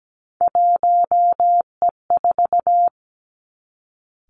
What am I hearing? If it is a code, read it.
Morse code, heard as a keyed tone, about 17 wpm, 700 Hz. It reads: 1E4
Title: Three Challenging Letter-Number Combinations 17wpm